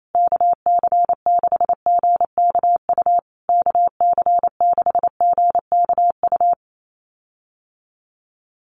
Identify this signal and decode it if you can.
Morse code, heard as a keyed tone, about 28 wpm, 700 Hz. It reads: KC6GKU KC6GKU
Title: USA Call Signs Repeated 3x 28wpm